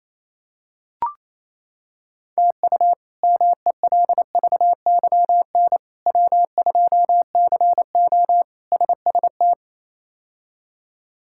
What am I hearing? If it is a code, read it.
Morse code, heard as a keyed tone, about 28 wpm, 700 Hz. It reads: TU MELVYN W2CO SST
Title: K1USN SST QSO Practice - 28wpm